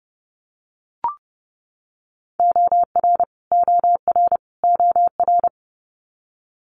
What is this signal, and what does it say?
Morse code, heard as a keyed tone, about 30 wpm, 700 Hz. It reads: OR OR OR
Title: Multi-letter Phonemes Repeated 3x 30wpm